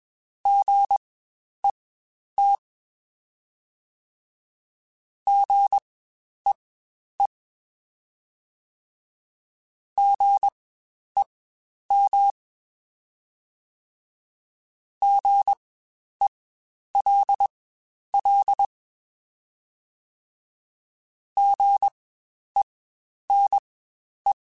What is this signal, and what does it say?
Morse code, heard as a keyed tone, about 20 wpm, spaced as for 10 wpm, 800 Hz. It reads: GET GEE GEM GELL GENE
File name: Spaced Alphas / Spaced Alphas mp3s / Gs sp @20